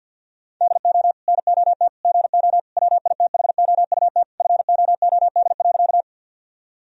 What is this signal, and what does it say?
Morse code, heard as a keyed tone, about 50 wpm, 700 Hz. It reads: DO NOT GO WITHOUT FOOD.
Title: Sentences from Top 1000 Words - Review Part 3 - 50wpm